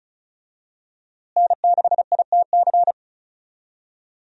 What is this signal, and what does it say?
Morse code, heard as a keyed tone, about 35 wpm, 700 Hz. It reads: N6ITC
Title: USA Call Signs - 5 Characters - Repeated 2x - 35wpm